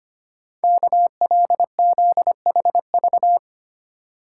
Morse code, heard as a keyed tone, about 25 wpm, 690 Hz. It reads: KLZHV
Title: Code Groups - 5 Letters - 25wpm